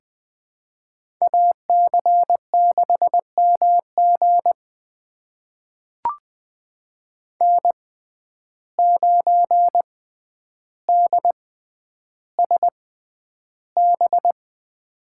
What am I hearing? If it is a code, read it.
Morse code, heard as a keyed tone, about 20 wpm, spaced as for 10 wpm, 700 Hz. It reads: AC6MG N9DSB
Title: USA Call Signs - 5 Characters - 6x Character Spacing - 20wpm